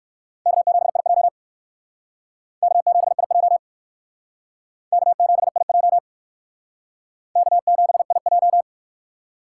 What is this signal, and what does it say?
Morse code, heard as a keyed tone, about 52 wpm, 700 Hz. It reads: K7IJ K7IJ K7IJ K7IJ
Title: Speed Racing - USA 1x2 Call Signs 45wpm